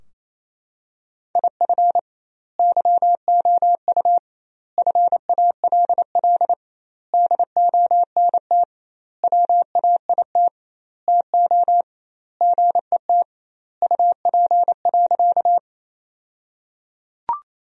Morse code, heard as a keyed tone, about 27 wpm, 700 Hz. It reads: IF YOU FALL DONT WAIT TO GET UP.
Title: 2x Word-Spacing - Sentences from Top 500 Words - 28wpm